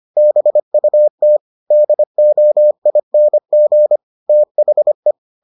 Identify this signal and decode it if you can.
Morse code, heard as a keyed tone, about 25 wpm, 600 Hz. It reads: BUT DOING THE